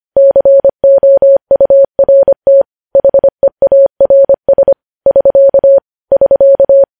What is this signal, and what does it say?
Morse code, heard as a keyed tone, about 25 wpm, 550 Hz. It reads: COURT HEARS <SK> <SK>